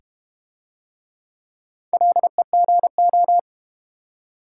Morse code, heard as a keyed tone, about 30 wpm, 700 Hz. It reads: LEGO
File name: Easy Words - Speed-Racing with Hints - 25wpm